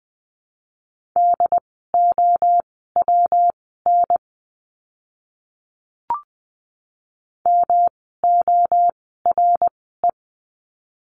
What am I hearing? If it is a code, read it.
Morse code, heard as a keyed tone, about 20 wpm, spaced as for 14 wpm, 700 Hz. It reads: DOWN MORE
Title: 192 Lesson 31f - Letter v - Words 20wpm